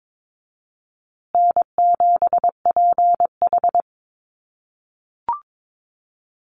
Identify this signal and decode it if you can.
Morse code, heard as a keyed tone, about 22 wpm, 700 Hz. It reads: N7PH
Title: USA Call Signs Repeated 3x 22wpm